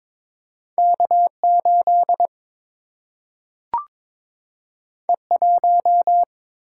Morse code, heard as a keyed tone, about 22 wpm, 700 Hz. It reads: K8 E1
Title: Single-Letter Single-Number 22wpm